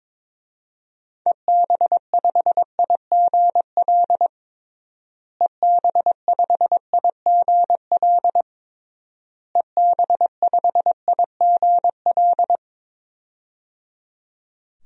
Morse code, heard as a keyed tone, about 22 wpm, 700 Hz. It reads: EB5IGL EB5IGL EB5IGL
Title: DXCC-100 Call Signs - 6 Characters - Repeated 3x - 22wpm